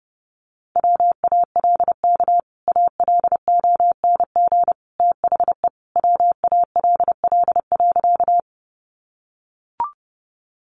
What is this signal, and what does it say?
Morse code, heard as a keyed tone, about 30 wpm, 700 Hz. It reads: WALK ALONG THE WALL.
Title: Sentences from Top 1100 Words 30wpm